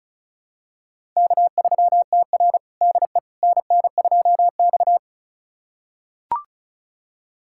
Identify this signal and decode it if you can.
Morse code, heard as a keyed tone, about 36 wpm, 700 Hz. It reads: K3TR DE NN2X
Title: USA Callsign DE Callsign - 4 Characters - 35wpm